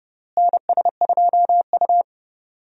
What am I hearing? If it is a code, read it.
Morse code, heard as a keyed tone, about 30 wpm, 700 Hz. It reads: NS2U
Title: USA Call Signs - 4 Characters - Repeated 3x - 30wpm